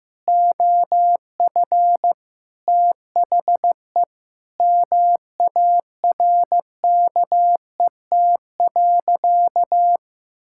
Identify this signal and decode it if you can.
Morse code, heard as a keyed tone, about 15 wpm, 700 Hz. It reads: OF THE MARKET.
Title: Sentences from Top 1000 Words - Review Part 6 - 15wpm